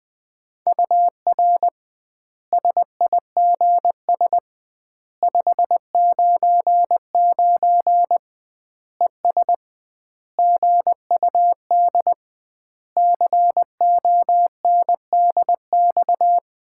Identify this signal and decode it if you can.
Morse code, heard as a keyed tone, about 21 wpm, 700 Hz. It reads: UR SIGS 599 ES GUD CONDX
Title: Common QSO Phrases - 2x Word-Spacing - 20wpm